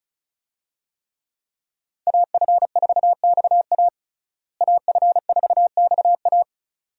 Morse code, heard as a keyed tone, about 34 wpm, 700 Hz. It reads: AF4XA AF4XA
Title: USA Call Signs - 5 Characters - Repeated 2x - 35wpm